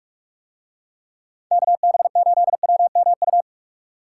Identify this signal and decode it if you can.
Morse code, heard as a keyed tone, about 43 wpm, 700 Hz. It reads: KB8WMU